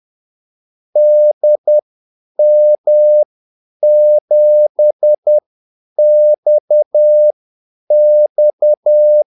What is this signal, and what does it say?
Morse code, heard as a keyed tone, about 10 wpm, spaced as for 8 wpm, 600 Hz. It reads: DM7XX
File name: Callsigns-10-8wpm